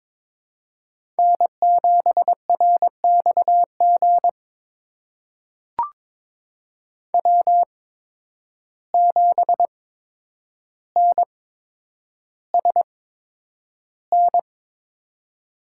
Morse code, heard as a keyed tone, about 22 wpm, spaced as for 9 wpm, 700 Hz. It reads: N7RXG W7NSN
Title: USA Call Signs - 5 Characters - 8x Character Spacing - 22wpm